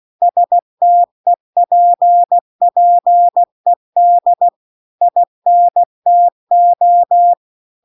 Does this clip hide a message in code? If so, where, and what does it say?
Yes: Morse code from 0.2 to 7.3 s, heard as a keyed tone, about 16 wpm, 700 Hz. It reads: STEPPED INTO